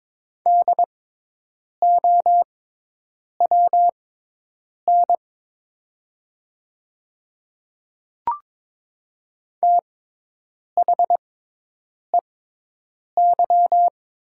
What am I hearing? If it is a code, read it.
Morse code, heard as a keyed tone, about 22 wpm, spaced as for 8 wpm, 700 Hz. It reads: DOWN THEY
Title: 6x Character Spacing - Top 100 Words - 22wpm